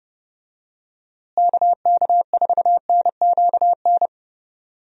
Morse code, heard as a keyed tone, about 30 wpm, 700 Hz. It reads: KK4NQN
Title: USA Call Signs - 6 Characters - Repeated 3x - 30wpm